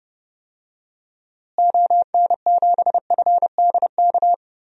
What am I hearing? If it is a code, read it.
Morse code, heard as a keyed tone, about 30 wpm, 700 Hz. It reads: ON7FDK
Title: DXCC-100 Call Signs - 6 Characters - Repeated 3x - 30wpm